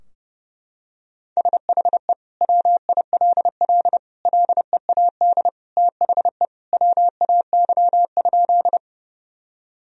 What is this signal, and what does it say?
Morse code, heard as a keyed tone, about 30 wpm, 700 Hz. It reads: SHE WILL LEAD THE WAY?
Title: Sentences from Top 1000 Words - Review Part 2 - 30wpm